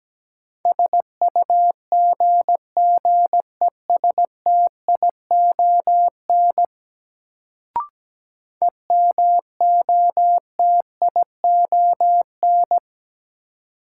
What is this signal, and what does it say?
Morse code, heard as a keyed tone, about 17 wpm, 700 Hz. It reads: SUGGESTION EMOTION
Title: Sentences from Top 1600 Words - New Word Drill - 17wpm